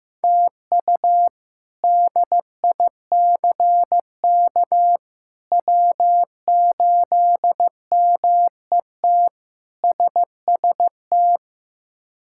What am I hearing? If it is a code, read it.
Morse code, heard as a keyed tone, about 15 wpm, 700 Hz. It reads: TU DICK W8MET SST
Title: K1USN SST QSO Practice - 15wpm